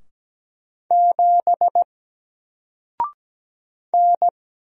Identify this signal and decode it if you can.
Morse code, heard as a keyed tone, about 17 wpm, 700 Hz. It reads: 7 N
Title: Single Letter-Number Alt 17wpm